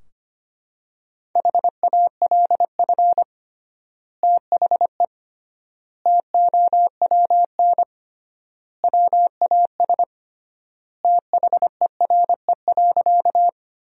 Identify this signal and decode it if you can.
Morse code, heard as a keyed tone, about 25 wpm, 700 Hz. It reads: HALF THE TOWN WAS THERE.
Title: 3x Word-Spacing - Sentences from Top 500 Words - 25wpm